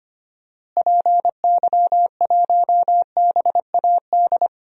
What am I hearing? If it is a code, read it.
Morse code, heard as a keyed tone, about 25 wpm, 700 Hz. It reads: PY1BAD
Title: DXCC-100 Call Signs - 6 Characters - Repeated 3x - 25wpm